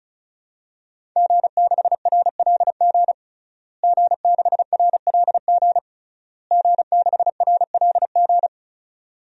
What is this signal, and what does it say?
Morse code, heard as a keyed tone, about 36 wpm, 690 Hz. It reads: G6RLG G6RLG G6RLG
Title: DXCC-100 Call Signs - 5 Characters - Repeated 3x - 35wpm